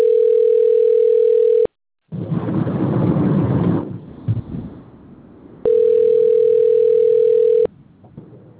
Ringback tone (North America), 440 + 480 Hz.